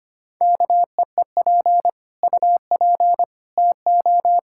Morse code, heard as a keyed tone, about 25 wpm, 700 Hz. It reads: KEEP UP TO
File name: Sets of 9 Words - 25wpm